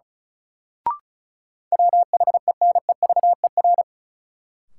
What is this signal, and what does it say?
Morse code, heard as a keyed tone, about 36 wpm, 700 Hz. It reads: WHENEVER
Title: Sentences from Top 1200 Words - New Word Drill Alt - 35wpm